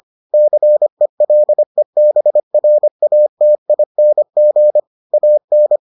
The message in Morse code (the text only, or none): CELEBRATING AN